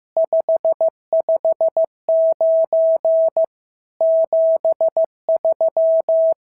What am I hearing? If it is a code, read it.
Morse code, heard as a keyed tone, about 14 wpm, 650 Hz. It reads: 559 73
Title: SOTA QSO Practice 15wpm